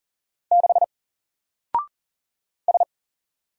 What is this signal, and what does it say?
Morse code, heard as a keyed tone, about 40 wpm, 700 Hz. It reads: 6 S
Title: Single Letter-Number Alt 40wpm